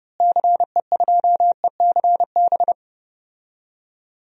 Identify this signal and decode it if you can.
Morse code, heard as a keyed tone, about 30 wpm, 700 Hz. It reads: CE2ECB
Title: DXCC-100 Call Signs - 6 Characters - Repeated 3x - 30wpm